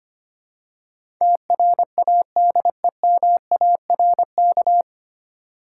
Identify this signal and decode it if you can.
Morse code, heard as a keyed tone, about 25 wpm, 700 Hz. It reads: TRADEMARK
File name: practice / 25wpm / 9 Letter Words - First 3 Letters Given - 25wpm